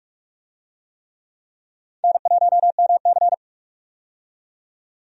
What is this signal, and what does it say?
Morse code, heard as a keyed tone, about 46 wpm, 700 Hz. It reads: N1MC
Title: USA Call Signs - 4 Characters - 45wpm